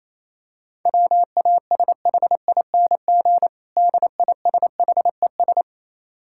Morse code, heard as a keyed tone, about 27 wpm, 700 Hz. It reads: WASHING DISHES
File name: Easy Words - Speed-Racing with Hints - 22wpm